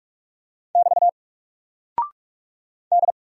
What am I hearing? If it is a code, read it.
Morse code, heard as a keyed tone, about 44 wpm, 700 Hz. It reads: <BT> D
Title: Single Character (Letter:Number:Punctuation:Prosign) 45wpm